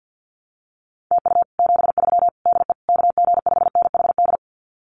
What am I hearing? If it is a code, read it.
Morse code, heard as a keyed tone, about 50 wpm, 700 Hz. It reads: TU 73 DE KG5NHD
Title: Common QSO Phrases Encore 50wpm